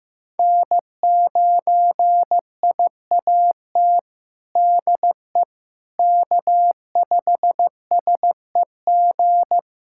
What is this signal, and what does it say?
Morse code, heard as a keyed tone, about 16 wpm, 700 Hz. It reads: N9IAT DE K5SEG
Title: USA Callsign DE Callsign - 5 Characters - 15wpm